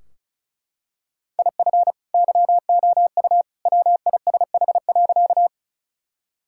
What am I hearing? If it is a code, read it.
Morse code, heard as a keyed tone, about 36 wpm, 700 Hz. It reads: IF YOU WISH.
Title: Sentences from Top 1600 Words 35wpm